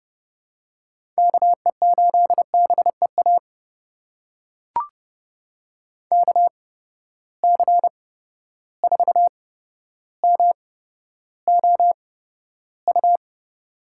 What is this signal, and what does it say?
Morse code, heard as a keyed tone, about 30 wpm, spaced as for 13 wpm, 700 Hz. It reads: KE8BEA KC4MOU
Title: USA Call Signs - 6 Characters - 8x Character Spacing - 30wpm